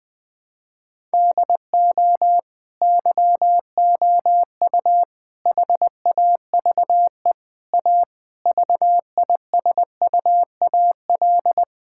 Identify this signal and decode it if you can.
Morse code, heard as a keyed tone, about 20 wpm, 700 Hz. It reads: DO YOU HAVE A VISUAL
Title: Sentences from Top 1000 Words - Review Part 6 - 20wpm